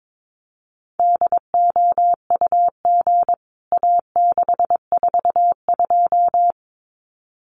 Morse code, heard as a keyed tone, about 22 wpm, 700 Hz. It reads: DOUG A642
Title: CWT Exchange 22wpm